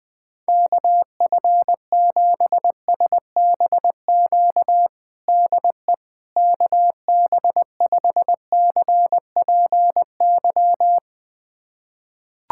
Morse code, heard as a keyed tone, about 20 wpm, 700 Hz. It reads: KF7SBQ DE KB5CPY
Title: USA Callsign DE Callsign 20wpm